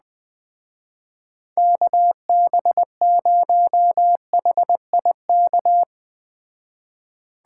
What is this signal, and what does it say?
Morse code, heard as a keyed tone, about 20 wpm, 700 Hz. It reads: KB0HIK
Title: Speed Racing - USA 2x3 Call Signs 17wpm